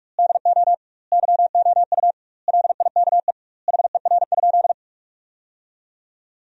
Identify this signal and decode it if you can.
Morse code, heard as a keyed tone, about 41 wpm, 700 Hz. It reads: DO YOU LIKE HER?
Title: 2x Word-Spacing - Sentences from Top 100 Words - 45wpm